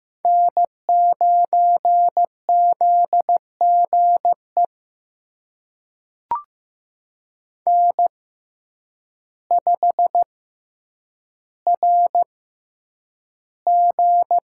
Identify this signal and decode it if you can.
Morse code, heard as a keyed tone, about 15 wpm, spaced as for 8 wpm, 700 Hz. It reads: N9ZGE N5RG
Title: USA Call Signs - 6x Character Spacing - 15wpm